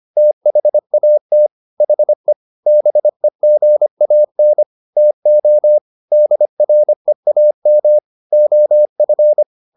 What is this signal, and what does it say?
Morse code, heard as a keyed tone, about 25 wpm, 600 Hz. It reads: THAT HE BEGAN TO DREAM OF